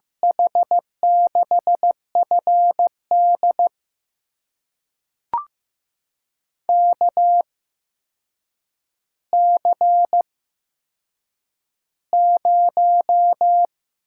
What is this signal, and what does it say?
Morse code, heard as a keyed tone, about 15 wpm, spaced as for 7 wpm, 700 Hz. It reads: H6FD KC0
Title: USA Call Signs - 8x Character Spacing - 15wpm